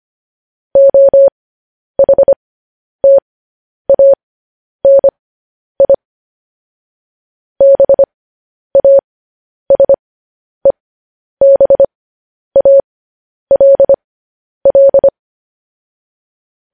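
Morse code, heard as a keyed tone, about 25 wpm, spaced as for 10 wpm, 550 Hz. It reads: OHTANI BASEBALL
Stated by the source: Morse news headlines at 10.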